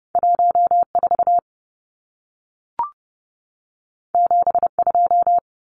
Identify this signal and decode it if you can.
Morse code, heard as a keyed tone, about 30 wpm, 700 Hz. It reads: 14 72
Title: Numbers - 2-Digit 30wpm